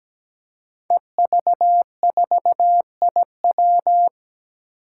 Morse code, heard as a keyed tone, about 17 wpm, 700 Hz. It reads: EV4IW
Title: Five Challenging Letter-Number Combinations 17wpm